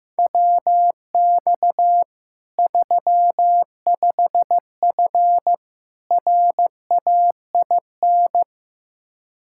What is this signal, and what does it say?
Morse code, heard as a keyed tone, about 15 wpm, 700 Hz. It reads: WX 35F RAIN
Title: K1USN SST QSO Practice 15wpm